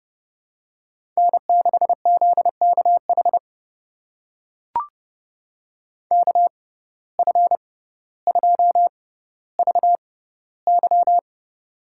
Morse code, heard as a keyed tone, about 30 wpm, spaced as for 15 wpm, 700 Hz. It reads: N6ZKH KF2VY
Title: USA Call Signs - 5 Characters - 6x Character Spacing - 30wpm